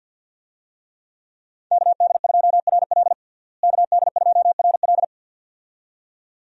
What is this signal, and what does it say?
Morse code, heard as a keyed tone, about 51 wpm, 700 Hz. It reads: KD2RL KD2RL
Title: USA Call Signs - 5 Characters - Repeated 2x - 50wpm